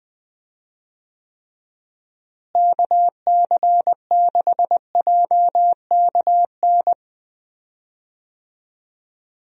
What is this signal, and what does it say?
Morse code, heard as a keyed tone, about 20 wpm, 700 Hz. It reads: KC6JKN
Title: USA Call Signs - 6 Characters - Repeated 2x - 20wpm